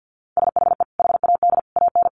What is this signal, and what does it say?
Morse code, heard as a keyed tone, about 50 wpm, 700 Hz. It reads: SHE HAD AN
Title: Sentences from Top 1400 Words 50wpm